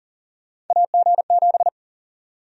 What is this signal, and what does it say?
Morse code, heard as a keyed tone, about 40 wpm, 700 Hz. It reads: AG7
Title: USA Call Sign Prefixes - Repeated 2x - 40wpm